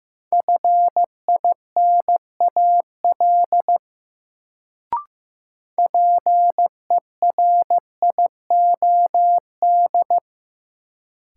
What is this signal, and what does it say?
Morse code, heard as a keyed tone, about 15 wpm, 700 Hz. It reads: FINAL PERIOD
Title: Sentences from Top 800 Words - New Word Drill - 15wpm